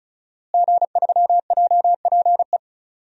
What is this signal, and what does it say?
Morse code, heard as a keyed tone, about 34 wpm, 700 Hz. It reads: G3JPE
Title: DXCC-100 Call Signs - 5 Characters - Repeated 3x - 35wpm